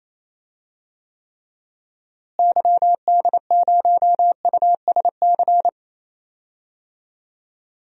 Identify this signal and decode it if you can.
Morse code, heard as a keyed tone, about 28 wpm, 700 Hz. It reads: YD0USC
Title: DXCC-100 Call Signs - 6 Characters - Repeated 2x - 28wpm